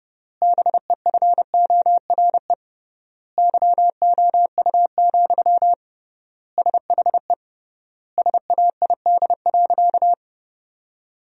Morse code, heard as a keyed tone, about 32 wpm, 700 Hz. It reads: BEFORE YOU, SHE SAID.
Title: 3x Word-Spacing - Sentences from Top 400 Words - 30wpm